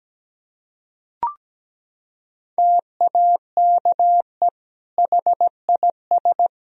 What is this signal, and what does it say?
Morse code, heard as a keyed tone, about 17 wpm, 700 Hz. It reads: TAKE HIS
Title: Sentences from Top 1000 Words - Review Part 3 - 17wpm